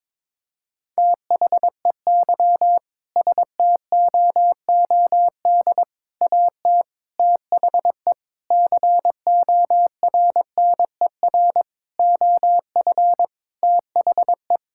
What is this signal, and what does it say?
Morse code, heard as a keyed tone, about 22 wpm, 700 Hz. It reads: THEY STOOD AT THE CORNER OF THE
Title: Sentences from Top 1600 Words 22wpm